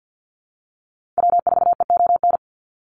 Morse code, heard as a keyed tone, about 50 wpm, 700 Hz. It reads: W4EON
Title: NA SOTA Activator Callsigns 50wpm